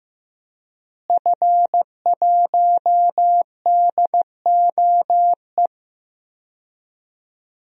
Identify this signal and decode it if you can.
Morse code, heard as a keyed tone, about 15 wpm, 700 Hz. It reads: F1DOE
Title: DXCC-100 Call Signs - 5 Characters - Repeated 2x - 15wpm